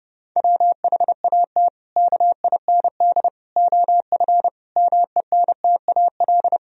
Morse code, heard as a keyed tone, about 30 wpm, 700 Hz. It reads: WHAT KIND OF MENTAL
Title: Sentences from Top 1000 Words - Review Part 4 - 30wpm